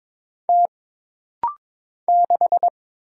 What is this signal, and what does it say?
Morse code, heard as a keyed tone, about 22 wpm, 700 Hz. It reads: T 6
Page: This rapid-fire timing is geared to push your ICR (Instant Character Recognition) skills.